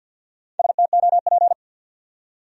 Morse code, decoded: STOP

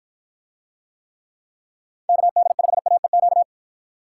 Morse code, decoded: KD5AEQ

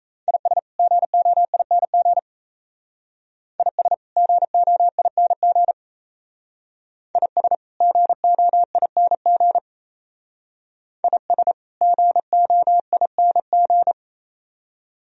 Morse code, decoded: IS GOING IS GOING IS GOING IS GOING

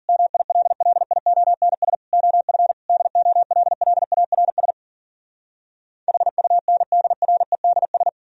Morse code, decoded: MILLIONS OF DOLLARS HUNDREDS

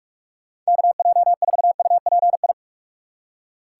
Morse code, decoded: KJ4UPI

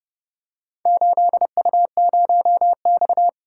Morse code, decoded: 8U0X

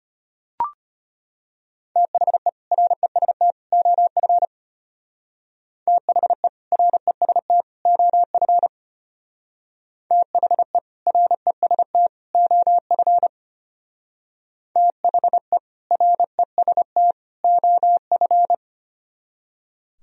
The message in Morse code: THE REST OF THE REST OF THE REST OF THE REST OF

